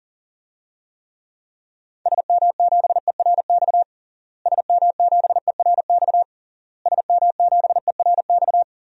Morse code, heard as a keyed tone, about 37 wpm, 700 Hz. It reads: SM7ERX SM7ERX SM7ERX
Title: DXCC-100 Call Signs - 6 Characters - Repeated 3x - 40wpm